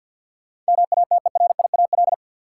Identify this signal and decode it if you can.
Morse code, heard as a keyed tone, about 52 wpm, 700 Hz. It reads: MATERIAL
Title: Sentences from Top 1100 Words - New Word Drill Alt - 50wpm